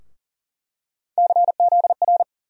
Morse code, heard as a keyed tone, about 40 wpm, 700 Hz. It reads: CZR
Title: USA Call Sign Postfixes - 3 Letters 40wpm